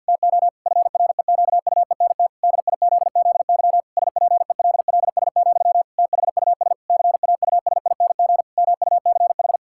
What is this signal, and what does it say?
Morse code, heard as a keyed tone, about 50 wpm, 700 Hz. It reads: TO FREQUENT DIZZY SPELLS, THUS CAUSING MUCH
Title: Pangram Challenge v2 - 50wpm